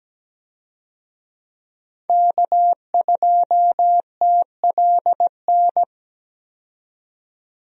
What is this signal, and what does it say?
Morse code, heard as a keyed tone, about 17 wpm, 700 Hz. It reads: K2TLN
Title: USA Call Signs - 5 Characters - 17wpm